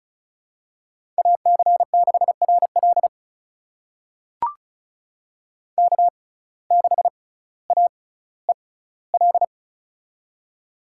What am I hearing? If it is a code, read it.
Morse code, heard as a keyed tone, about 36 wpm, spaced as for 18 wpm, 700 Hz. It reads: AC6RL K6AEL
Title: USA Call Signs - 5 Characters - 6x Character Spacing - 35wpm